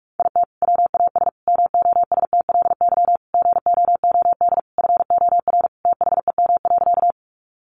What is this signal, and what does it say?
Morse code, heard as a keyed tone, about 44 wpm, 700 Hz. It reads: IT WAS MOSTLY GOOD FOR THEM.
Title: Sentences from Top 1300 Words 45wpm